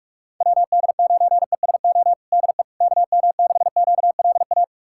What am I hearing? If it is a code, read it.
Morse code, heard as a keyed tone, about 42 wpm, 700 Hz. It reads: WD9ESO DE KM6QLA
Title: USA Callsign DE Callsign 45wpm